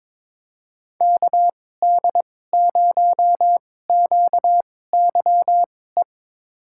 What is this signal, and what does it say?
Morse code, heard as a keyed tone, about 21 wpm, spaced as for 16 wpm, 700 Hz. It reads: KD0QYE
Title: USA Call Signs - 6 Characters - 2x Character Spacing - 22wpm